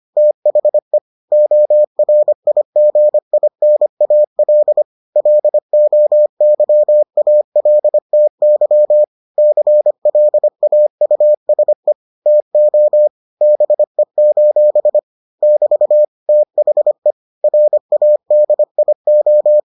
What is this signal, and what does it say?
Morse code, heard as a keyed tone, about 25 wpm, 600 Hz. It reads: THE ORIGINAL LOYALTY CLAUSE TO BE: <BT> THE RADIO